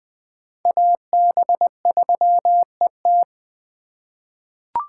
Morse code, heard as a keyed tone, about 20 wpm, 700 Hz. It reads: AB3ET
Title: USA Call Signs - 3x Character Spacing - 20wpm